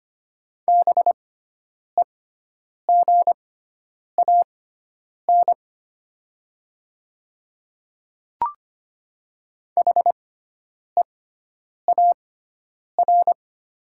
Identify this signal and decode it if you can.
Morse code, heard as a keyed tone, about 25 wpm, spaced as for 9 wpm, 700 Hz. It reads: BEGAN HEAR
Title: 6x Character Spacing - Top 300 Words - 25wpm